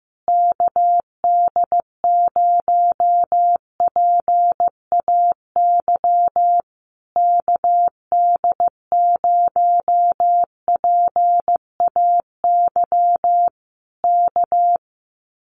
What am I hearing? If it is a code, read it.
Morse code, heard as a keyed tone, about 15 wpm, 700 Hz. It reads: KD0PAY KD0PAY K
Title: Common QSO Phrases Encore 15wpm